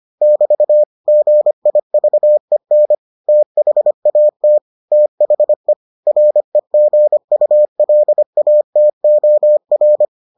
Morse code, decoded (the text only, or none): <BT> GIVEN THAT THE REGULATOR